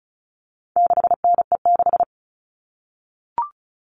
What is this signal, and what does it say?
Morse code, heard as a keyed tone, about 34 wpm, 700 Hz. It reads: 6NE6
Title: All Cut Numbers - 4-Digit 35wpm